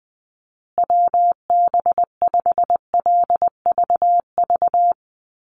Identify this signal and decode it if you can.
Morse code, heard as a keyed tone, about 19 wpm, 700 Hz. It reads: WB5LVV
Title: USA Call Signs Repeated 2x 20wpm